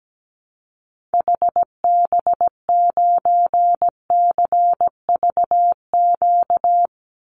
Morse code, heard as a keyed tone, about 17 wpm, 700 Hz. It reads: HB9CVQ
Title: Call Signs Repeated 3x - 17wpm